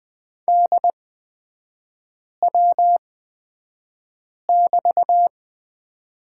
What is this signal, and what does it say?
Morse code, heard as a keyed tone, about 20 wpm, spaced as for 5 wpm, 700 Hz. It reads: DW<BT>
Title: 236 Lesson 38e - Letter q - Three Characters 20wpm